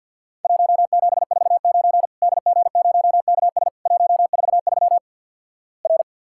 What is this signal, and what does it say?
Morse code, heard as a keyed tone, about 48 wpm, 700 Hz. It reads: 1749 DG0KS 143 R
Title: Worked All Europe DX Contest - QTC Exchange - 50wpm